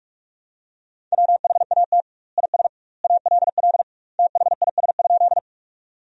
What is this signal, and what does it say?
Morse code, heard as a keyed tone, about 41 wpm, 700 Hz. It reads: WHAT IS ALL THIS?
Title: 2x Word-Spacing - Sentences from Top 100 Words - 45wpm